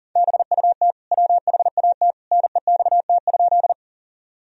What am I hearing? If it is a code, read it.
Morse code, heard as a keyed tone, about 40 wpm, 700 Hz. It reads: BUT WHAT NEXT?
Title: Sentences from Top 1000 Words - Review Part 1 - 40wpm